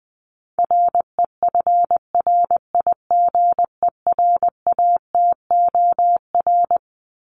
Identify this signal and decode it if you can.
Morse code, heard as a keyed tone, about 20 wpm, 700 Hz. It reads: REFRIGERATOR
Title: Top 1501-1750 Words 20wpm